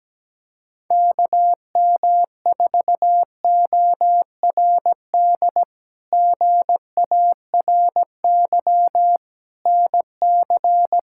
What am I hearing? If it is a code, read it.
Morse code, heard as a keyed tone, about 17 wpm, 700 Hz. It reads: KM4ORD GARY NC
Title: K1USN SST QSO Practice - 17wpm